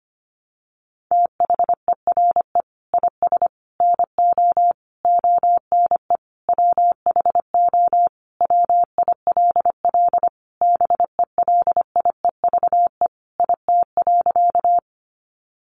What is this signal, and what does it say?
Morse code, heard as a keyed tone, about 25 wpm, 700 Hz. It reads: THERE IS NO ONE WHO WILL BELIEVE IT.
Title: Sentences from Top 1000 Words 25wpm